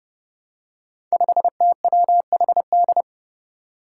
Morse code, heard as a keyed tone, about 30 wpm, 700 Hz. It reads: 5TWHD
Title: Five Challenging Letter-Number Combinations 30wpm